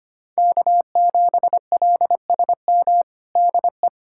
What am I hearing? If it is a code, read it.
Morse code, heard as a keyed tone, about 25 wpm, 700 Hz. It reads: K7LSM DE
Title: USA Callsign DE Callsign 25wpm